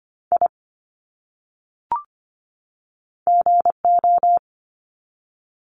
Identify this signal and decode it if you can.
Morse code, heard as a keyed tone, about 25 wpm, 700 Hz. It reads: I GO
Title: Top 300 Words 25wpm